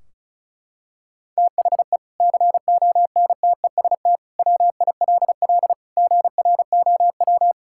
Morse code, heard as a keyed tone, about 34 wpm, 700 Hz. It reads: THE CONTEST WILL GROW
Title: Sentences from Top 1600 Words 35wpm